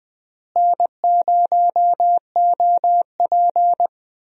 Morse code, heard as a keyed tone, about 21 wpm, 700 Hz. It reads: N0OP
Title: USA Call Signs - 4 Characters - Repeated 2x - 20wpm